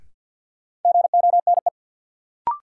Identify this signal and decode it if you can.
Morse code, heard as a keyed tone, about 53 wpm, 700 Hz. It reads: GONE
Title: Sentences from Top 1200 Words - New Word Drill Alt - 50wpm